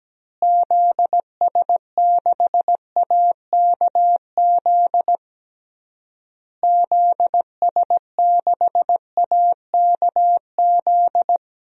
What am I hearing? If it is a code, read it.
Morse code, heard as a keyed tone, about 17 wpm, 700 Hz. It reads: ZS6AKZ ZS6AKZ
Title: DXCC-100 Call Signs - 6 Characters - Repeated 3x - 17wpm